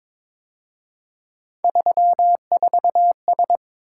Morse code, heard as a keyed tone, about 27 wpm, 700 Hz. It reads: 34S
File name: Speed-Racing - Three Letter-Number Combinations - 22wpm